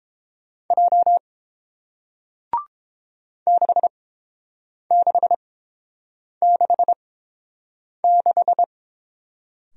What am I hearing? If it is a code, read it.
Morse code, heard as a keyed tone, about 30 wpm, spaced as for 7 wpm, 700 Hz. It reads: J 6666